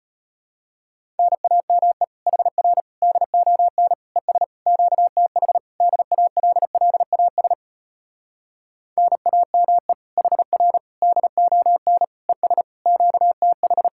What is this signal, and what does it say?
Morse code, heard as a keyed tone, about 33 wpm, 700 Hz. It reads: NAME HR DON ES QTH DALLAS NAME HR DON ES QTH